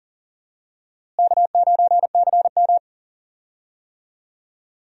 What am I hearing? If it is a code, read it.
Morse code, heard as a keyed tone, about 40 wpm, 700 Hz. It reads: K9CM
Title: USA Call Signs - 4 Characters - Repeated 2x - 40wpm